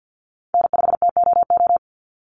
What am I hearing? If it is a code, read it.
Morse code, heard as a keyed tone, about 50 wpm, 700 Hz. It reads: N5TOO
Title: Call Signs - 50wpm